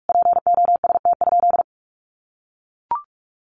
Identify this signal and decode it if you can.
Morse code, heard as a keyed tone, about 44 wpm, 700 Hz. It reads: POST?
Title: Sentences from Top 1000 Words 45wpm